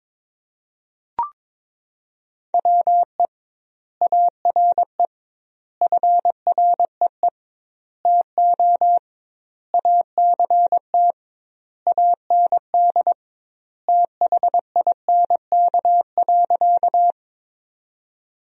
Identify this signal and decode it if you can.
Morse code, heard as a keyed tone, about 22 wpm, 700 Hz. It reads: WE ARE FREE TO ACT AND THINK.
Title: 2x Word-Spacing - Sentences from Top 500 Words - 22wpm